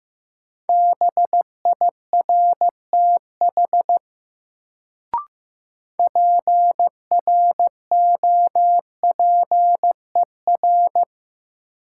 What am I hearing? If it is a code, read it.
Morse code, heard as a keyed tone, about 15 wpm, 700 Hz. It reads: BIRTH PROPER
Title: Sentences from Top 800 Words - New Word Drill - 15wpm